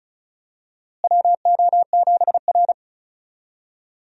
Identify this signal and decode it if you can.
Morse code, heard as a keyed tone, about 34 wpm, 700 Hz. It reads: WO7R
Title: USA Call Signs - 4 Characters - Repeated 3x - 35wpm